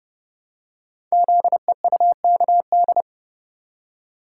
Morse code, heard as a keyed tone, about 30 wpm, 700 Hz. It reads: ZEUKD
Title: Code Groups - 5 Letters - 30wpm